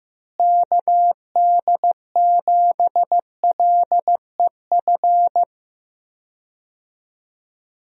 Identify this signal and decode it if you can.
Morse code, heard as a keyed tone, about 15 wpm, 700 Hz. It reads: KD7LEF
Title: USA Call Signs - 6 Characters - Repeated 2x - 15wpm